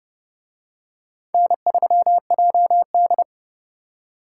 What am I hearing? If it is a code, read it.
Morse code, heard as a keyed tone, about 29 wpm, 700 Hz. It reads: N3JD